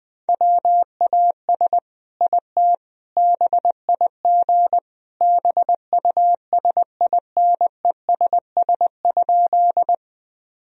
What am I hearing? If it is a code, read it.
Morse code, heard as a keyed tone, about 20 wpm, 700 Hz. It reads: WAS IT BIG BUSINESS?
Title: Sentences from Top 1000 Words - Review Part 1 - 20wpm